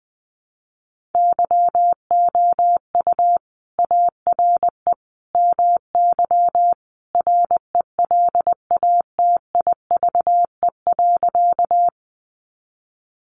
Morse code, heard as a keyed tone, about 20 wpm, 700 Hz. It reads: YOU ARE MY RELATIVE.
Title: Sentences from Top 1400 Words 20wpm